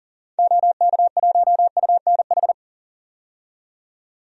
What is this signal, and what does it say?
Morse code, heard as a keyed tone, about 40 wpm, 700 Hz. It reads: OK1UNH
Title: DXCC-100 Call Signs - 6 Characters - 40wpm